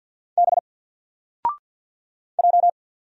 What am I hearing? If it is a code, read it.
Morse code, heard as a keyed tone, about 50 wpm, 700 Hz. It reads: B J
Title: Single Letter-Number Alt 50wpm